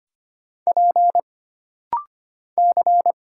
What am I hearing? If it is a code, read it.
Morse code, heard as a keyed tone, about 25 wpm, 700 Hz. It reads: P C
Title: Single Letter-Number Alt - Rapid-Fire 25wpm